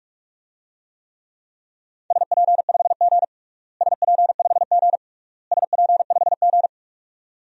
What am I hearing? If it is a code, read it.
Morse code, heard as a keyed tone, about 46 wpm, 700 Hz. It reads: SP5G SP5G SP5G
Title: DXCC-100 Call Signs - 4 Characters - Repeated 3x - 45wpm